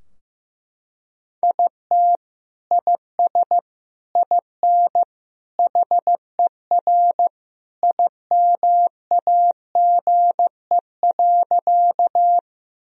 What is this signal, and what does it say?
Morse code, heard as a keyed tone, about 15 wpm, 700 Hz. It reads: IT IS IN HER IMAGE.
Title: Sentences from Top 1000 Words - Review Part 4 - 15wpm